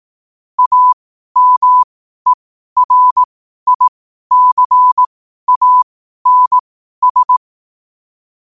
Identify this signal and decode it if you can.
Morse code, heard as a keyed tone, about 18 wpm, spaced as for 12 wpm, 1000 Hz. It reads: AMERICANS
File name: qotdfiles / 12wpm